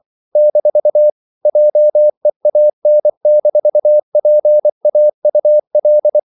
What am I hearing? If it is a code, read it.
Morse code, heard as a keyed tone, about 24 wpm, 600 Hz. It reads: - JEAN-PAUL
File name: CW / Practice / EnglishQuotations0324